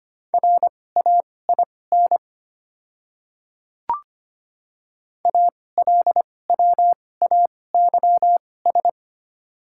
Morse code, heard as a keyed tone, about 25 wpm, spaced as for 18 wpm, 700 Hz. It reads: RAIN ALWAYS
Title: 2x Character Spacing - Top 300 Words - 25wpm